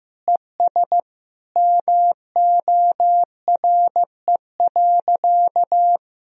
Morse code, heard as a keyed tone, about 15 wpm, 700 Hz. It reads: ES MORE.
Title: Sentences from Top 500 Words and Abbreviations 15wpm